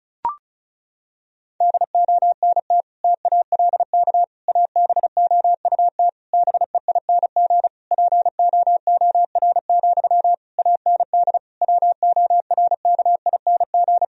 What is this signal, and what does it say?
Morse code, heard as a keyed tone, about 36 wpm, 700 Hz. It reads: DONT TALK ABOUT BEING POOR, AND WORKING
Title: Sentences from Top 1000 Words - Review Part 4 - 35wpm